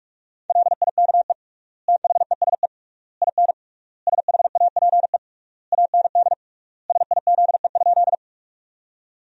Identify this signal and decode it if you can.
Morse code, heard as a keyed tone, about 46 wpm, 700 Hz. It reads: LIKE THESE IN SHAPE AND SIZE?
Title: 3x Word-Spacing - Sentences from Top 500 Words - 45wpm